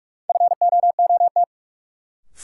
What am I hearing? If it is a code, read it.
Morse code, heard as a keyed tone, about 46 wpm, 700 Hz. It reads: FOOT
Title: Sentences from Top 500 Words - New Word Drill - 45wpm